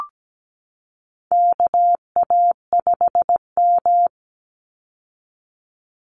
Morse code, heard as a keyed tone, about 17 wpm, 700 Hz. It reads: KA5M
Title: Call Signs - 17wpm